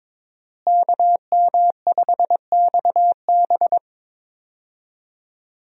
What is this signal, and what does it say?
Morse code, heard as a keyed tone, about 22 wpm, 700 Hz. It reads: KM5XB
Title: USA Call Signs - 5 Characters - Repeated 3x - 22wpm